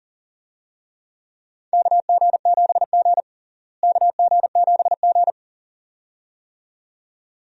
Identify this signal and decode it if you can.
Morse code, heard as a keyed tone, about 40 wpm, 700 Hz. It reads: KG7G KG7G
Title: USA Call Signs - 4 Characters - Repeated 2x - 40wpm